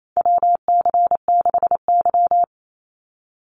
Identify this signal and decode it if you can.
Morse code, heard as a keyed tone, about 28 wpm, 700 Hz. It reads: WC6Y
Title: Call Signs Repeated 2x - 28wpm